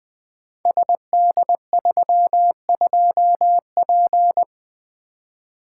Morse code, heard as a keyed tone, about 20 wpm, 700 Hz. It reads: SD32P
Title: Code Groups - 5 Letter:Numbers - 20wpm